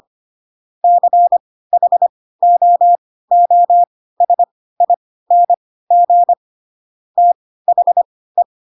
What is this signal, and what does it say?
Morse code, heard as a keyed tone, about 25 wpm, spaced as for 16 wpm, 700 Hz. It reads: CHOOSING THE
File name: Pinocchio / Pinocchio ch3 10 25 ik1hge